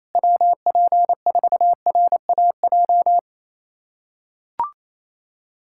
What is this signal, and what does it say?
Morse code, heard as a keyed tone, about 28 wpm, 700 Hz. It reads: WP4RAJ
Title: USA Callsign DE Callsign - 6 Characters - 28wpm